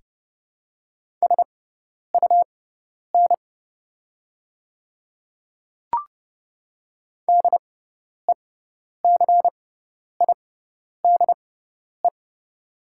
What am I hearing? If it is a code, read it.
Morse code, heard as a keyed tone, about 30 wpm, spaced as for 10 wpm, 690 Hz. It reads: SUN DECIDE
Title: 6x Character Spacing - Top 500 Words - 30wpm